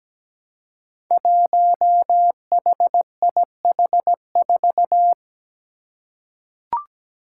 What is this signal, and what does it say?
Morse code, heard as a keyed tone, about 17 wpm, 700 Hz. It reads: 1HIH4
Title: Five Challenging Letter-Number Combinations 17wpm